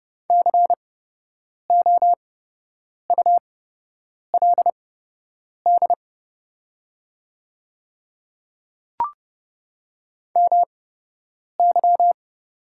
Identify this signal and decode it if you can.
Morse code, heard as a keyed tone, about 30 wpm, spaced as for 8 wpm, 700 Hz. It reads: COULD MY
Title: 8x Character Spacing - Top 100 Words - 30wpm